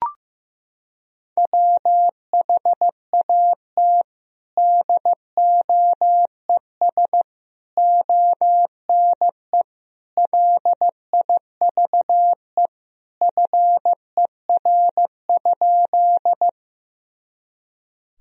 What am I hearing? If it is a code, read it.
Morse code, heard as a keyed tone, about 15 wpm, 700 Hz. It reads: WHAT DOES ONE LIVE FER?
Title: Sentences from Top 200 Words and Abbreviations 15wpm